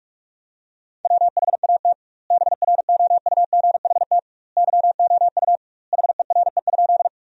Morse code, heard as a keyed tone, about 46 wpm, 700 Hz. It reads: WHAT BROUGHT YOU HERE?
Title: 2x Word-Spacing - Sentences from Top 500 Words - 45wpm